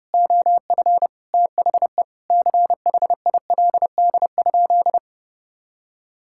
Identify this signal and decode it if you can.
Morse code, heard as a keyed tone, about 30 wpm, 700 Hz. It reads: OF THE CHILD?
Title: Sentences from Top 1600 Words 30wpm